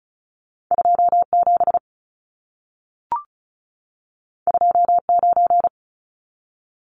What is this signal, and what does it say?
Morse code, heard as a keyed tone, about 36 wpm, 700 Hz. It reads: 27 29
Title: Numbers - 2-Digit 35wpm